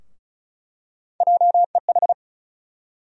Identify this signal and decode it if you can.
Morse code, heard as a keyed tone, about 36 wpm, 700 Hz. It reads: JEH
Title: USA Call Sign Postfixes - 3 Letters 35wpm